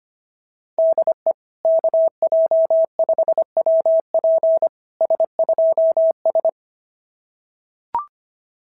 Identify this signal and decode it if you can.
Morse code, heard as a keyed tone, about 24 wpm, 650 Hz. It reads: DE KJ5WP S2S
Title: SOTA QSO Practice 25wpm